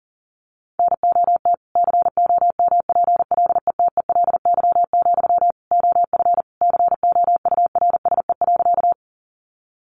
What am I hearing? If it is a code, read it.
Morse code, heard as a keyed tone, about 40 wpm, 700 Hz. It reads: NOT COMPLETELY, OF COURSE.
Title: Sentences from Top 1100 Words 40wpm